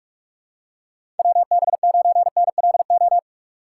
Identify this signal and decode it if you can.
Morse code, heard as a keyed tone, about 46 wpm, 700 Hz. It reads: WB9NLO
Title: USA Call Signs - 6 Characters - 4x Character Spacing - 45wpm